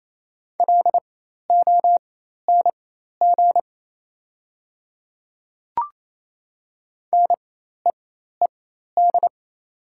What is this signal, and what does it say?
Morse code, heard as a keyed tone, about 28 wpm, spaced as for 13 wpm, 700 Hz. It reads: LONG NEED
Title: 4x Character Spacing - Top 500 Words - 28wpm